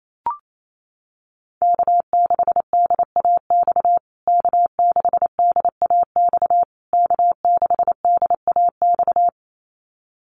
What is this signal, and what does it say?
Morse code, heard as a keyed tone, about 28 wpm, 700 Hz. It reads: K6DAX K6DAX K6DAX
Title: USA Call Signs Repeated 3x 28wpm